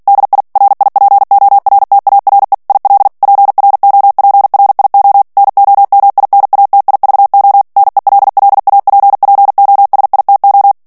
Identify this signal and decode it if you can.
Morse code, heard as a keyed tone, about 48 wpm, 800 Hz. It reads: DI RIPORTARE IL PROPRIO NOMINATIVO NELLAPPOSITO
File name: ITALIEN texte clair 2 vitesse 45 mots minute